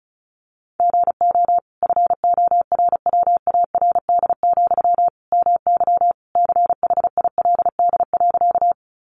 Morse code, heard as a keyed tone, about 34 wpm, 700 Hz. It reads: GO FORWARD, MY CHILD.
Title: Sentences from Top 1300 Words 35wpm